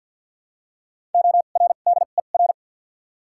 Morse code, heard as a keyed tone, about 53 wpm, spaced as for 37 wpm, 700 Hz. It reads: ORDER
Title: 2x Character Spacing - Top 500 Words - 50wpm